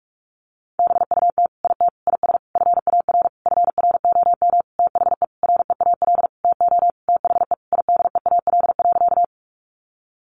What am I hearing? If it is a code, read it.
Morse code, heard as a keyed tone, about 44 wpm, 700 Hz. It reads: BUT IT IS FAR FROM THE REAL TO THE IDEAL.
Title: Sentences from Top 1400 Words 45wpm